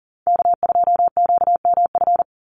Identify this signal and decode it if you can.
Morse code, heard as a keyed tone, about 40 wpm, 700 Hz. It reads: K2QMF
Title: Call Signs Repeated 2x - 40wpm